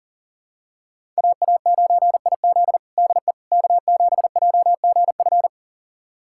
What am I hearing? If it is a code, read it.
Morse code, heard as a keyed tone, about 40 wpm, 700 Hz. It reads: AA9IZ DE K7JGF
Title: USA Callsign DE Callsign - 5 Characters - 40wpm